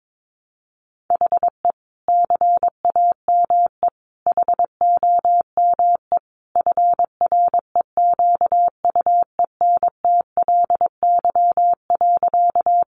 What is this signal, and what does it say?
Morse code, heard as a keyed tone, about 22 wpm, 700 Hz. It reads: HE CAME HOME FREQUENTLY.
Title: Sentences from Top 1300 Words 22wpm